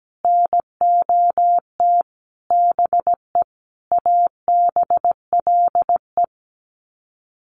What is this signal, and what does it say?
Morse code, heard as a keyed tone, about 17 wpm, 700 Hz. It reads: NOT BE ABLE
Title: Sets of 5 Words Encore2 17wpm